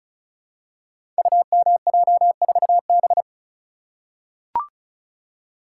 Morse code, heard as a keyed tone, about 36 wpm, 700 Hz. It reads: UMJ4B
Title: Code Groups - 5 Letter:Numbers - 35wpm